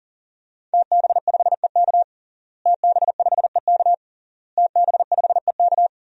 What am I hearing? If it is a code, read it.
Morse code, heard as a keyed tone, about 40 wpm, 700 Hz. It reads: TB5EK TB5EK TB5EK
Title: DXCC-100 Call Signs - 5 Characters - Repeated 3x - 40wpm